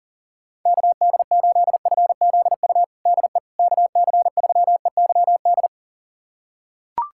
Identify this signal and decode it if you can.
Morse code, heard as a keyed tone, about 40 wpm, 700 Hz. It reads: KD8FZU DE KC3EYD
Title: USA Callsign DE Callsign 40wpm